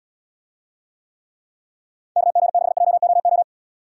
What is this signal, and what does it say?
Morse code, heard as a keyed tone, about 76 wpm, 700 Hz. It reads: KG6JOO